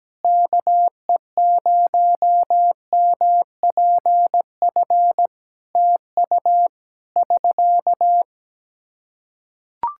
Morse code, heard as a keyed tone, about 17 wpm, 700 Hz. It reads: KE0MPF TU <SK>
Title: Practice QSO - Exchange 4 - Ending - 17wpm